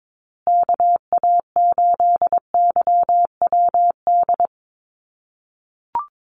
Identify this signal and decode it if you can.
Morse code, heard as a keyed tone, about 22 wpm, 700 Hz. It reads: KA8YWD
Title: USA Call Signs Repeated 2x 22wpm